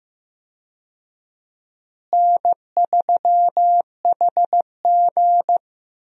Morse code, heard as a keyed tone, about 15 wpm, 700 Hz. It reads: N3HG
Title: USA Call Signs - 4 Characters - Repeated 3x - 15wpm